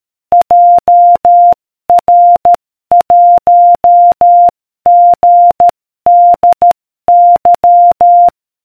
Morse code, heard as a keyed tone, about 13 wpm, 700 Hz. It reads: JR1GDY